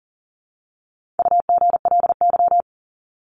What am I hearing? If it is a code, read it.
Morse code, heard as a keyed tone, about 40 wpm, 700 Hz. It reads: UGLY
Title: Top 1751-2000 Words 40wpm